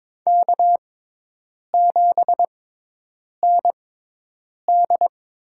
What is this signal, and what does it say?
Morse code, heard as a keyed tone, about 22 wpm, spaced as for 8 wpm, 700 Hz. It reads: K7ND
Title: USA Call Signs - 5 Characters - 6x Character Spacing - 22wpm